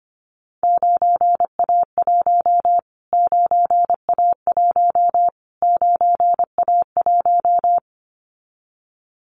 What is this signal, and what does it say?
Morse code, heard as a keyed tone, about 25 wpm, 700 Hz. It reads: 9A1 9A1 9A1
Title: Callsign Prefixes for Top 25 DXCC - Repeated 3x - 25wpm